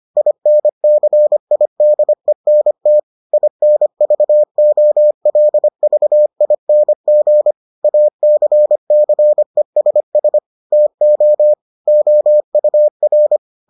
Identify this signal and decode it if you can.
Morse code, heard as a keyed tone, about 25 wpm, 600 Hz. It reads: INCIDENT INVOLVING ACCESS TO OUR